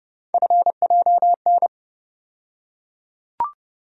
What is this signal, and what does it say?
Morse code, heard as a keyed tone, about 30 wpm, 700 Hz. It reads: FJN
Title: USA Call Sign Postfixes - 3 Letters 30wpm